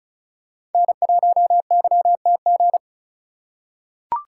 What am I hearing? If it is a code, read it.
Morse code, heard as a keyed tone, about 36 wpm, 700 Hz. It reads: N1YTG
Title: USA Call Signs - 5 Characters - 2x Character Spacing - 35wpm